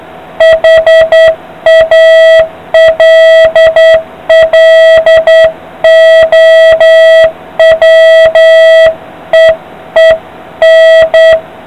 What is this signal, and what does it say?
Morse code, heard as a keyed tone, about 9 wpm, 660 Hz, with noise. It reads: HALLOWEEN